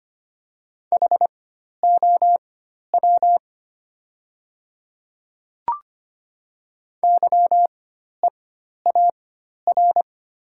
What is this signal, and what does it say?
Morse code, heard as a keyed tone, about 25 wpm, spaced as for 12 wpm, 700 Hz. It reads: HOW YEAR
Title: 4x Character Spacing - Top 300 Words - 25wpm